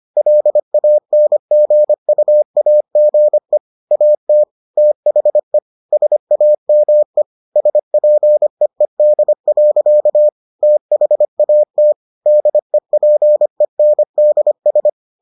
Morse code, heard as a keyed tone, about 25 wpm, 600 Hz. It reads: LANGUAGE AT THE SAME SPEED. THAT DEPENDS